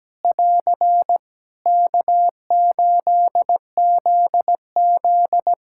Morse code, heard as a keyed tone, about 17 wpm, 700 Hz. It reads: <AR> K8ZZ
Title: Practice QSO - Exchange 4 - Ending - 17wpm